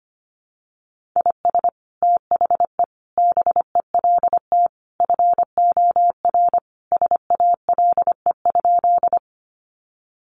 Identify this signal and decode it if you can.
Morse code, heard as a keyed tone, about 25 wpm, 700 Hz. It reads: IS THE BELT FOR SALE?
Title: Sentences from Top 1400 Words 25wpm